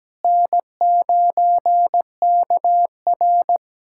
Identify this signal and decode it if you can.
Morse code, heard as a keyed tone, about 17 wpm, 700 Hz. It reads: N9KR
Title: K1USN SST QSO Practice - 17wpm